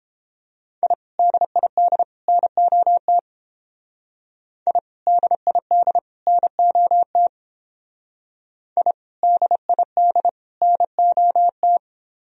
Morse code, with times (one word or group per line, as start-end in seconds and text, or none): 0.8-0.9 I
1.2-2.0 DID
2.3-3.2 NOT
4.7-4.8 I
5.1-6.0 DID
6.3-7.3 NOT
8.8-8.9 I
9.2-10.3 DID
10.6-11.8 NOT